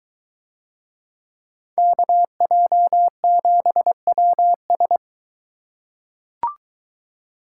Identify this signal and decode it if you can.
Morse code, heard as a keyed tone, about 20 wpm, 700 Hz. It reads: KJ7WS